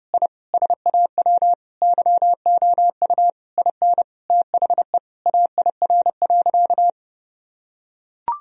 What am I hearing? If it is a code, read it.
Morse code, heard as a keyed tone, about 30 wpm, 700 Hz. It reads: I SAW YOU IN THE AIR.
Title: Sentences from Top 1000 Words - Review Part 1 - 30wpm